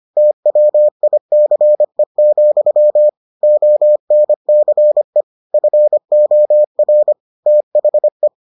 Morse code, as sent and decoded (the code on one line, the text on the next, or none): - .-- .. -.-. --..-- / --- -. -.-. / ..-. --- .-. / - .... .
TWICE, ONCE FOR THE